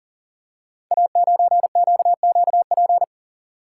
Morse code, heard as a keyed tone, about 38 wpm, 700 Hz. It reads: A9QQP
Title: Code Groups - 5 Letter:Numbers - 40wpm